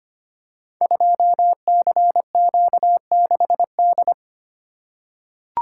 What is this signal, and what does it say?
Morse code, heard as a keyed tone, about 25 wpm, 700 Hz. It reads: 2CQ6D
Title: Code Groups - 5 Letter:Numbers - 25wpm